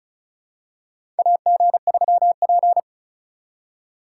Morse code, heard as a keyed tone, about 36 wpm, 700 Hz. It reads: AG3P
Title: USA Call Signs - 4 Characters - 4x Character Spacing - 35wpm